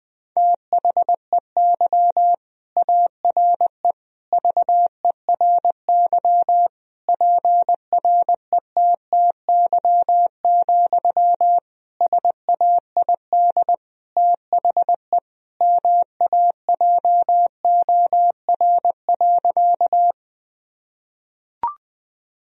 Morse code, heard as a keyed tone, about 20 wpm, 700 Hz. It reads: THEY ARE VERY PRETTY, SAID THE MAJOR.
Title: Sentences from Top 1000 Words - Review Part 4 - 20wpm